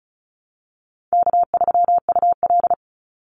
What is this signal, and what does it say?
Morse code, heard as a keyed tone, about 34 wpm, 700 Hz. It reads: K3UL
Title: Easy Call Signs Repeated 3x - 35wpm